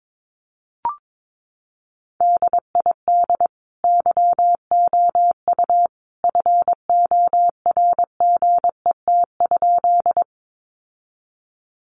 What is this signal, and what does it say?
Morse code, heard as a keyed tone, about 22 wpm, 700 Hz. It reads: DID YOU FORGET?
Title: Sentences from Top 1200 Words 22wpm